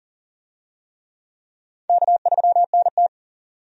Morse code, heard as a keyed tone, about 40 wpm, 700 Hz. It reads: K3NT
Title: USA Call Signs - 4 Characters - 40wpm